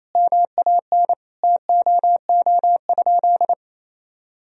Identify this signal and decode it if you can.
Morse code, heard as a keyed tone, about 28 wpm, 700 Hz. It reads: MAN TOO?
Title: Sentences from Top 1000 Words - Review Part 4 - 28wpm